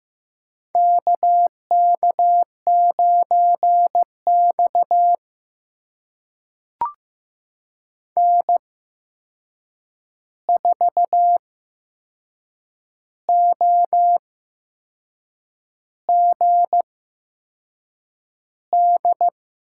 Morse code, heard as a keyed tone, about 15 wpm, spaced as for 6 wpm, 700 Hz. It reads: KK9X N4OGD
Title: USA Call Signs - 8x Character Spacing - 15wpm